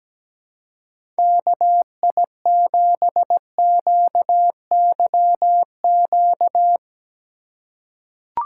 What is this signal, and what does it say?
Morse code, heard as a keyed tone, about 17 wpm, 700 Hz. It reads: KI7QYQ
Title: USA Call Signs - 3x Character Spacing - 17wpm